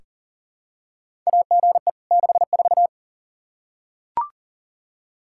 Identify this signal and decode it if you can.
Morse code, heard as a keyed tone, about 40 wpm, 700 Hz. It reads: AGE 64
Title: Practice QSO - Exchange 3 - More Info - 40wpm